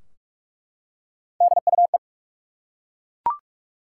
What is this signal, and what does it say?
Morse code, heard as a keyed tone, about 42 wpm, 700 Hz. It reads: DUE
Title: Three Letter Words - 45wpm